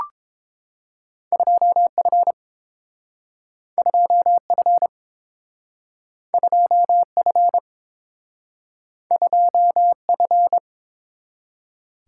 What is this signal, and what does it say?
Morse code, heard as a keyed tone, about 30 wpm, 700 Hz. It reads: 2F 2F 2F 2F